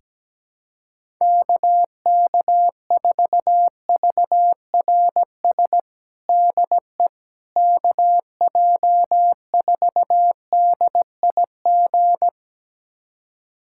Morse code, heard as a keyed tone, about 17 wpm, 700 Hz. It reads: KK4VRS DE KJ4DIG
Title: USA Callsign DE Callsign - 6 Characters - 17wpm